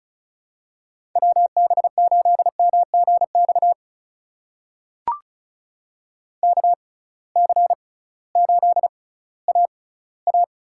Morse code, heard as a keyed tone, about 36 wpm, spaced as for 19 wpm, 700 Hz. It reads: WB8MGX KC8AA
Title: USA Call Signs - 6 Characters - 6x Character Spacing - 35wpm